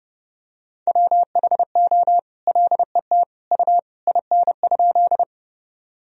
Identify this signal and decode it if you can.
Morse code, heard as a keyed tone, about 32 wpm, 700 Hz. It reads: WHO LET U IN?
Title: Sentences from Top 300 Words and Abbreviations 30wpm